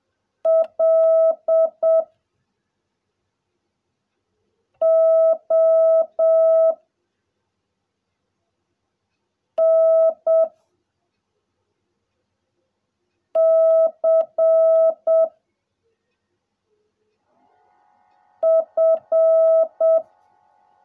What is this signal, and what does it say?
Morse code, heard as a keyed tone, about 7 wpm, 660 Hz. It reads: L O N C F
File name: jh8cbh / 2021cw / cw16a